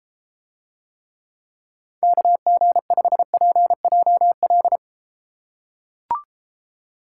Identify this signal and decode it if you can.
Morse code, heard as a keyed tone, about 33 wpm, 700 Hz. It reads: KG5PJL